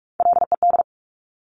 Morse code, heard as a keyed tone, about 46 wpm, 700 Hz. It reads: LED
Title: Top Three Letter Combinations in Words 45wpm